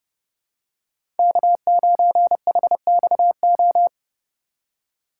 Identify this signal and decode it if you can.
Morse code, heard as a keyed tone, about 30 wpm, 700 Hz. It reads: K9HXO
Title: USA Call Signs - 5 Characters - 4x Character Spacing - 30wpm